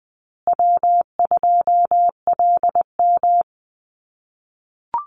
Morse code, heard as a keyed tone, about 20 wpm, 700 Hz. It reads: W2LM
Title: USA Call Signs Repeated 3x 20wpm